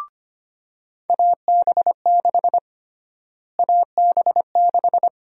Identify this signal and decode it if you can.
Morse code, heard as a keyed tone, about 25 wpm, 700 Hz. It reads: AB6 AB6
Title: USA Call Sign Prefixes - Repeated 2x - 25wpm